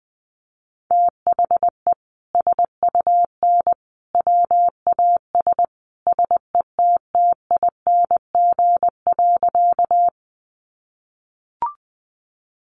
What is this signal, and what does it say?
Morse code, heard as a keyed tone, about 19 wpm, 700 Hz. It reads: THE SUN WAS SETTING.
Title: Sentences from Top 1300 Words 20wpm